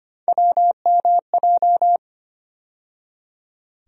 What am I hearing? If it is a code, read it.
Morse code, heard as a keyed tone, about 25 wpm, 700 Hz. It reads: WMJ
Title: USA Call Sign Postfixes - 3 Letters 25wpm